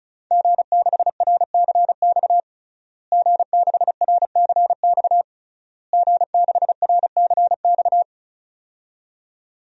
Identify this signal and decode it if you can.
Morse code, heard as a keyed tone, about 36 wpm, 700 Hz. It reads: G6RCX G6RCX G6RCX
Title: DXCC-100 Call Signs - 5 Characters - Repeated 3x - 35wpm